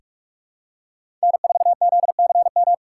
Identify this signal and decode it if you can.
Morse code, heard as a keyed tone, about 46 wpm, 700 Hz. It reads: N4ZCM
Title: USA Call Signs - 2x Character Spacing - 45wpm